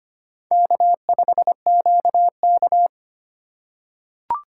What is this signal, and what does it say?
Morse code, heard as a keyed tone, about 25 wpm, 700 Hz. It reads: K5QK
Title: USA Call Signs - 4 Characters - 3x Character Spacing - 25wpm